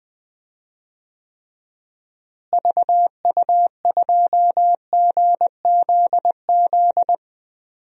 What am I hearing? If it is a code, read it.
Morse code, heard as a keyed tone, about 20 wpm, 700 Hz. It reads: VU2GZZ
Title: DXCC-100 Call Signs - 6 Characters - Repeated 3x - 20wpm